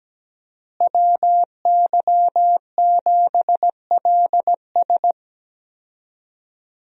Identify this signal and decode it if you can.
Morse code, heard as a keyed tone, about 17 wpm, 700 Hz. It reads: WY7LS
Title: USA Call Signs - 5 Characters - Repeated 2x - 17wpm